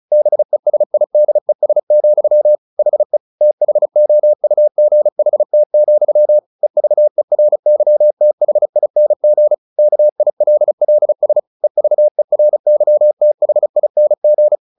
Morse code, heard as a keyed tone, about 34 wpm, 600 Hz. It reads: BESIDES, HE THOUGHT, EVERYTHING KILLS EVERYTHING
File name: the old man and the sea 2 35wpm